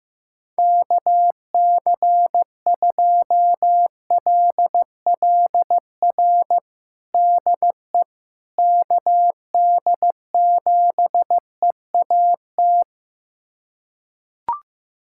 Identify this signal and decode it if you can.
Morse code, heard as a keyed tone, about 15 wpm, 700 Hz. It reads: KC2LLR DE KD7EAT
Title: USA Callsign DE Callsign - 6 Characters - 15wpm